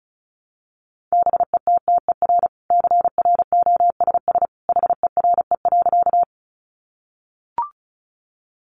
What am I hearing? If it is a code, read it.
Morse code, heard as a keyed tone, about 34 wpm, 700 Hz. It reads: BETTER CROSS HERE.
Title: Sentences from Top 400 Words 35wpm